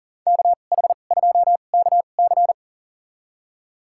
Morse code, heard as a keyed tone, about 40 wpm, spaced as for 29 wpm, 700 Hz. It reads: KH2KC
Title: USA Call Signs - 2x Character Spacing - 40wpm